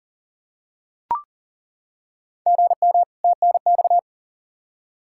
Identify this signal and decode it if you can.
Morse code, heard as a keyed tone, about 40 wpm, 700 Hz. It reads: GM TNX
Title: K1USN SST QSO Practice 40wpm